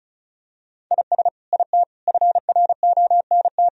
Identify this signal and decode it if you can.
Morse code, heard as a keyed tone, about 36 wpm, 700 Hz. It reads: IS IT FRONT
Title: Sentences from Top 1000 Words - Review Part 2 - 35wpm